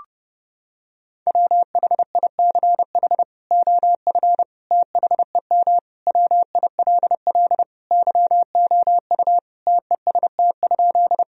Morse code, heard as a keyed tone, about 30 wpm, 700 Hz. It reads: WHICH OF THEM WILL YOU TEST?
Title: Sentences from Top 1000 Words - Review Part 3 - 30wpm